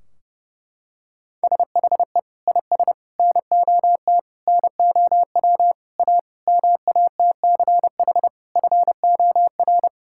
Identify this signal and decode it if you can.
Morse code, heard as a keyed tone, about 32 wpm, 700 Hz. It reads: SHE IS NOT NOW A MATCH FOR
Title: Sentences from Top 1600 Words 30wpm